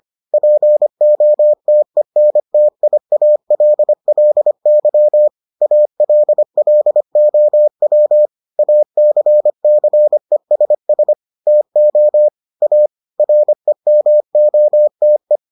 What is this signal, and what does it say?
Morse code, heard as a keyed tone, about 25 wpm, 600 Hz. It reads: POTENTIALLY ALLOW ACCESS TO A REMOTE